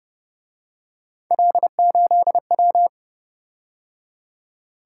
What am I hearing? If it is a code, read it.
Morse code, heard as a keyed tone, about 28 wpm, 700 Hz. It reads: L8W